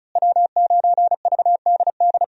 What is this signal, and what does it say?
Morse code, heard as a keyed tone, about 36 wpm, 700 Hz. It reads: W9VDD
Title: USA Call Signs - 5 Characters - Repeated 2x - 35wpm